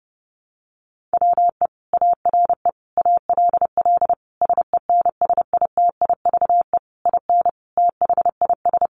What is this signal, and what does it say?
Morse code, heard as a keyed tone, about 28 wpm, 700 Hz. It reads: WE ARE ALL SENSITIVE IN THIS
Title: Sentences from Top 1400 Words 30wpm